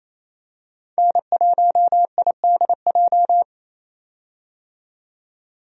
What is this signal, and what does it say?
Morse code, heard as a keyed tone, about 28 wpm, 700 Hz. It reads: N1IDJ
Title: USA Call Signs - 5 Characters - 28wpm